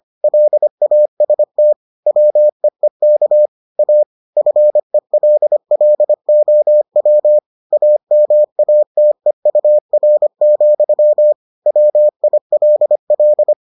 Morse code, heard as a keyed tone, about 25 wpm, 600 Hz. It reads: LAST WEEK A FELLOW AMATEUR, WILL